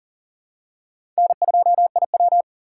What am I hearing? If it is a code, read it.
Morse code, heard as a keyed tone, about 40 wpm, 700 Hz. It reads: N2IW
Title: USA Call Signs - 4 Characters - 2x Character Spacing - 40wpm